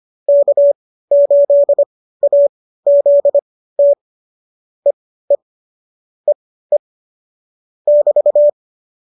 Morse code, heard as a keyed tone, about 25 wpm, spaced as for 15 wpm, 570 Hz. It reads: K8AZT EE EE <BT>